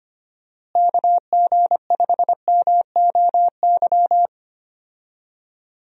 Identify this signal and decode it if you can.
Morse code, heard as a keyed tone, about 25 wpm, 700 Hz. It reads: KG5MOY
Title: USA Call Signs - 6 Characters - Repeated 2x - 25wpm